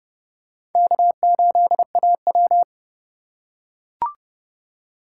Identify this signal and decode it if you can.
Morse code, heard as a keyed tone, about 29 wpm, 700 Hz. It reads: K8AW